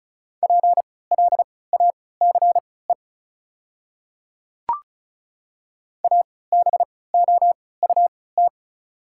Morse code, heard as a keyed tone, about 36 wpm, spaced as for 20 wpm, 690 Hz. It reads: PLACE ABOUT
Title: 3x Character Spacing - Top 200 Words - 35wpm